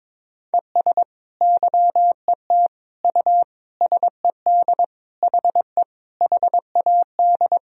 Morse code, heard as a keyed tone, about 22 wpm, 700 Hz. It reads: ES YET U SED HE HAD
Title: Sentences from Top 500 Words and Abbreviations 22wpm